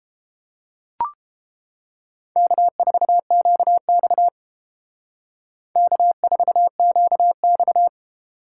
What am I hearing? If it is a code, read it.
Morse code, heard as a keyed tone, about 29 wpm, 700 Hz. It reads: K4QX K4QX